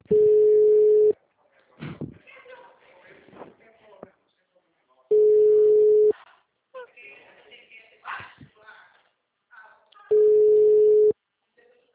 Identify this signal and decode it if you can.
Ringback tone (Europe, 425 Hz), with speech.